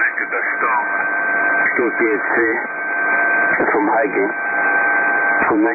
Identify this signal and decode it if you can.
Emergency Alert System (SAME) tones (header bursts), heard as harsh buzzing bursts, with speech. It is heard from 0.9 to 5.4 s.